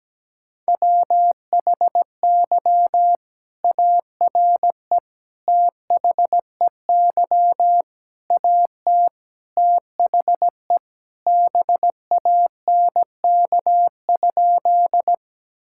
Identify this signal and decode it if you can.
Morse code, heard as a keyed tone, about 17 wpm, 700 Hz. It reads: WHY ARE THEY AT THE BANK?
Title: Sentences from Top 1000 Words - Review Part 6 - 17wpm